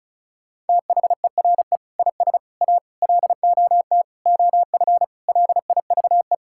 Morse code, heard as a keyed tone, about 36 wpm, 700 Hz. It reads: THERE IS A LOT OF LIVE
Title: Sentences from Top 1000 Words - Review Part 5 - 35wpm